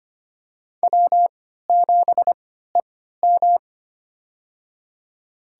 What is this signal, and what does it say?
Morse code, heard as a keyed tone, about 25 wpm, spaced as for 14 wpm, 700 Hz. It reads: W7EM
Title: USA Call Signs - 4 Characters - 3x Character Spacing - 25wpm